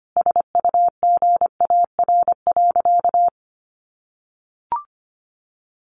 Morse code, heard as a keyed tone, about 25 wpm, 700 Hz. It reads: SUGAR.
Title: Sentences from Top 1200 Words 25wpm